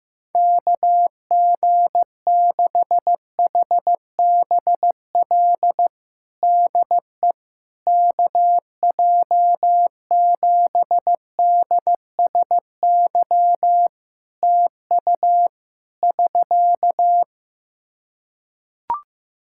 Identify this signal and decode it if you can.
Morse code, heard as a keyed tone, about 15 wpm, 700 Hz. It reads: KG6HBL DE KJ7DSY TU <SK>
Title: K1USN SST QSO Practice 15wpm